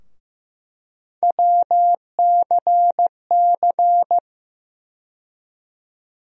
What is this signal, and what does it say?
Morse code, heard as a keyed tone, about 15 wpm, 700 Hz. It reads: WCC
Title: USA Call Sign Postfixes - 3 Letters 15wpm